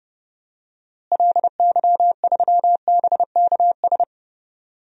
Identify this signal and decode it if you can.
Morse code, heard as a keyed tone, about 32 wpm, 700 Hz. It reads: LY3BKS
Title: DXCC-100 Call Signs - 6 Characters - Repeated 3x - 30wpm